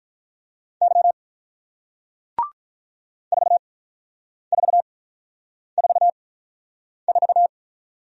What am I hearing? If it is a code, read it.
Morse code, heard as a keyed tone, about 48 wpm, 700 Hz. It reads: Y 4 4 4 4